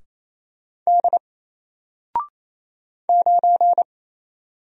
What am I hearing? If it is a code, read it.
Morse code, heard as a keyed tone, about 28 wpm, 700 Hz. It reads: D 9
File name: Single-Letter-Number-28wpm